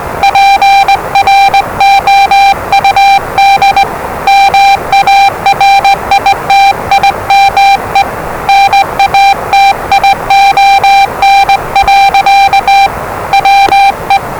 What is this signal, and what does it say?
Morse code, heard as a keyed tone, about 20 wpm, 790 Hz, with noise. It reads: PROUD MARITIME NATION. WE
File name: WellingtonRadioZlwCloseDownOnHfAndMf / 01-Zlw 4277khz 417.5 qrt until 0925z